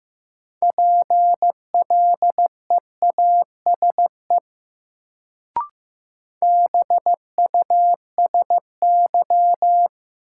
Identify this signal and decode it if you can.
Morse code, heard as a keyed tone, about 15 wpm, 700 Hz. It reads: PLEASE BUSY
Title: Sentences from Top 1500 Words - New Word Drill Alt - 15wpm